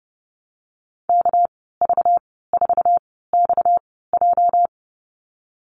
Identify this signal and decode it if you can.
Morse code, heard as a keyed tone, about 30 wpm, spaced as for 17 wpm, 700 Hz. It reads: KV4XJ
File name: Lesson 37g - Letter x - Call Signs 30wpm